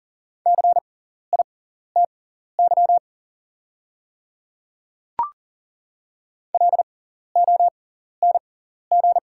Morse code, heard as a keyed tone, about 40 wpm, spaced as for 14 wpm, 700 Hz. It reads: CITY LONG
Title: 6x Character Spacing - Top 300 Words - 40wpm